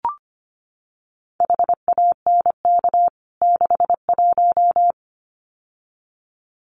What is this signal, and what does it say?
Morse code, heard as a keyed tone, about 25 wpm, 700 Hz. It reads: HANK 61
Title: CWT Exchange 25wpm